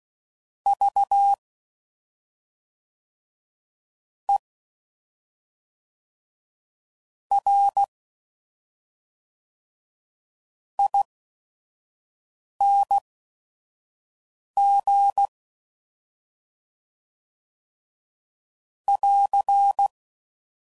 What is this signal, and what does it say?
Morse code, heard as a keyed tone, about 16 wpm, spaced as for 5 wpm, 800 Hz. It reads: V E R ING <AR>
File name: Suffixes / Special ING's / SlowToFaster ING's